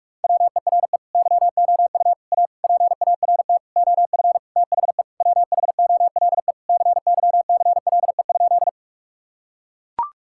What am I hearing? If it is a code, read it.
Morse code, heard as a keyed tone, about 41 wpm, 700 Hz. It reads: WERE YOU A PART OF THE WHOLE CYCLE?
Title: Sentences from Top 1000 Words - Review Part 6 - 45wpm